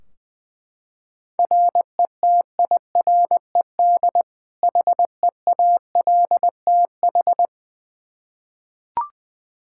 Morse code, heard as a keyed tone, about 20 wpm, 700 Hz. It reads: RETIRED HEALTH
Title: Practice QSO - Exchange 3 - More Info - 20wpm